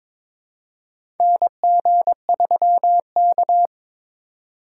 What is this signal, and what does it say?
Morse code, heard as a keyed tone, about 22 wpm, 700 Hz. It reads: NG3K
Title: USA Call Signs - 4 Characters - 2x Character Spacing - 22wpm